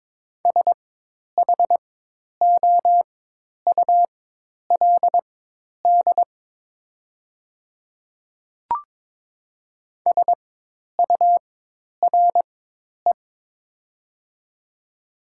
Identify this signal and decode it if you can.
Morse code, heard as a keyed tone, about 22 wpm, spaced as for 10 wpm, 700 Hz. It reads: SHOULD SURE
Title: 4x Character Spacing - Top 300 Words - 22wpm